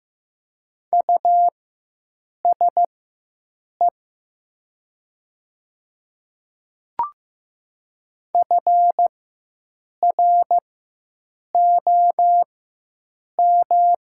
Morse code, heard as a keyed tone, about 15 wpm, spaced as for 7 wpm, 700 Hz. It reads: USE FROM
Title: 4x Character Spacing - Top 200 Words - 15wpm